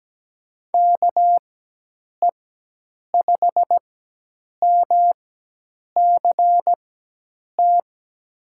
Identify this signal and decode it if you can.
Morse code, heard as a keyed tone, about 17 wpm, spaced as for 8 wpm, 700 Hz. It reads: KE5MCT
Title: USA Call Signs - 6 Characters - 4x Character Spacing - 17wpm